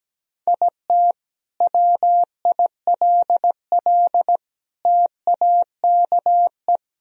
Morse code, heard as a keyed tone, about 17 wpm, 700 Hz. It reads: IT WILL TAKE
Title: Sentences from Top 1000 Words - Review Part 3 - 17wpm